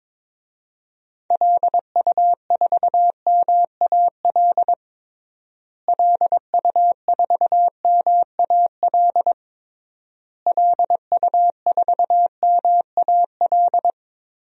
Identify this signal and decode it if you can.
Morse code, heard as a keyed tone, about 22 wpm, 700 Hz. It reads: LU4MAL LU4MAL LU4MAL
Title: DXCC-100 Call Signs - 6 Characters - Repeated 3x - 22wpm